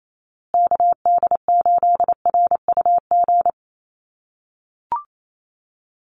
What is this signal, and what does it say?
Morse code, heard as a keyed tone, about 28 wpm, 700 Hz. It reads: KD8RUG
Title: USA Call Signs Repeated 2x 28wpm